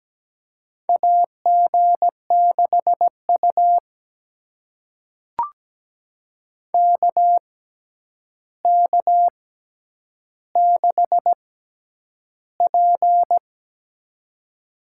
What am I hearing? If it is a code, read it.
Morse code, heard as a keyed tone, about 17 wpm, spaced as for 9 wpm, 700 Hz. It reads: AG6U KK6P
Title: USA Call Signs - 4 Characters - 6x Character Spacing - 17wpm